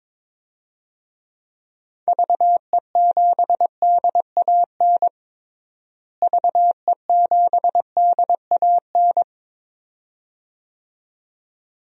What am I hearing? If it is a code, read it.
Morse code, heard as a keyed tone, about 22 wpm, 700 Hz. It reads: VE7DAN VE7DAN
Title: DXCC-100 Call Signs - 6 Characters - Repeated 2x - 22wpm